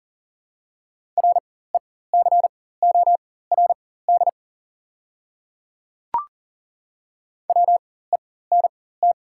Morse code, heard as a keyed tone, about 40 wpm, spaced as for 19 wpm, 700 Hz. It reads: RECORD WENT
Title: 4x Character Spacing - Top 500 Words - 40wpm